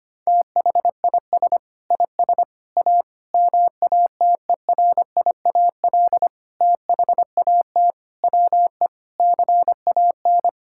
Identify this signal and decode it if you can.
Morse code, heard as a keyed tone, about 25 wpm, 700 Hz. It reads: THIS IS A MATERIAL THAT WE CAN
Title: Sentences from Top 1000 Words - Review Part 5 - 25wpm